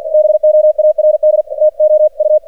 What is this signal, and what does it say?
Morse code, heard as a keyed tone, about 49 wpm, 610 Hz. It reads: BOARD TO U
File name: archived / CRSnet